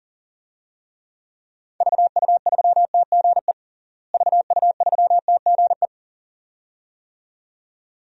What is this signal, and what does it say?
Morse code, heard as a keyed tone, about 40 wpm, 700 Hz. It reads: VU3TGE VU3TGE
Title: DXCC-100 Call Signs - 6 Characters - Repeated 2x - 40wpm